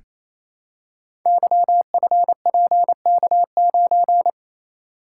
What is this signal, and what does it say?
Morse code, heard as a keyed tone, about 29 wpm, 700 Hz. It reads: YFPK9
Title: Code Groups - 5 Letter:Numbers - 28wpm